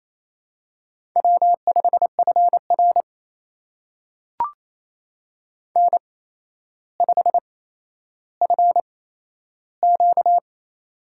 Morse code, heard as a keyed tone, about 27 wpm, spaced as for 12 wpm, 700 Hz. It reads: W5FR N5FQ
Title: USA Call Signs - 4 Characters - 8x Character Spacing - 28wpm